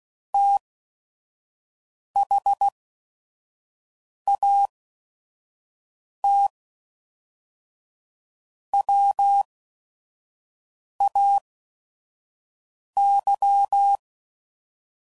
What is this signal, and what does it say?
Morse code, heard as a keyed tone, about 16 wpm, spaced as for 5 wpm, 800 Hz. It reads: THAT WAY
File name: CW Code Groups @5WPM / QSOs / QSO WA7QOX 1